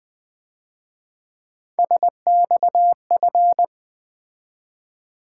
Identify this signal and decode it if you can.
Morse code, heard as a keyed tone, about 23 wpm, 700 Hz. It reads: SXF
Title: Speed-Racing - Three Letter-Number Combinations - 20wpm